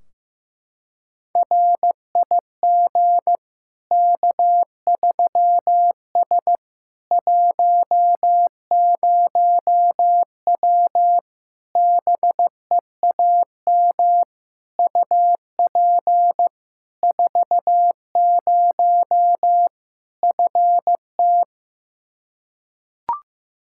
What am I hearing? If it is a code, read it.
Morse code, heard as a keyed tone, about 15 wpm, 700 Hz. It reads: RIG K3S 10W BEAM UP 40 FT
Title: Practice QSO - Exchange 2 - Info - 15wpm